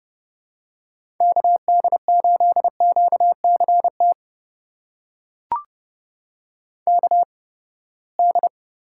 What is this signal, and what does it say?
Morse code, heard as a keyed tone, about 30 wpm, spaced as for 21 wpm, 700 Hz. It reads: KD8QCT KD
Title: USA Call Signs - 6 Characters - 8x Character Spacing - 30wpm